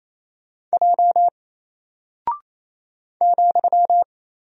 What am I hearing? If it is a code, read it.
Morse code, heard as a keyed tone, about 28 wpm, 700 Hz. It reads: J ,
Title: Single Character (Letter:Number:Punctuation:Prosign) Alt 28wpm